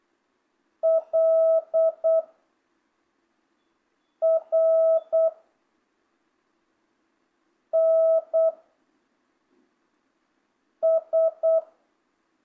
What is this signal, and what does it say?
Morse code, heard as a keyed tone, about 8 wpm, 660 Hz. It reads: L R N S
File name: jh8cbh / 2025cw